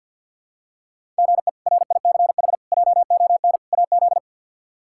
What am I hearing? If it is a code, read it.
Morse code, heard as a keyed tone, about 46 wpm, 700 Hz. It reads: GE RICH JON AZ
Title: K1USN SST QSO Practice - 50wpm